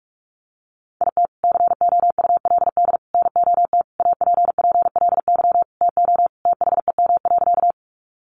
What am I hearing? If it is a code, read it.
Morse code, heard as a keyed tone, about 44 wpm, 700 Hz. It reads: IT COULD NOT APPLY TO THEM.
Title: Sentences from Top 1200 Words 45wpm